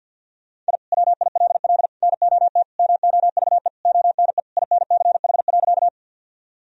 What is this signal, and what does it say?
Morse code, heard as a keyed tone, about 51 wpm, 700 Hz. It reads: I WILL NOT MOVE ONE INCH.
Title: Sentences from Top 1000 Words - Review Part 3 - 50wpm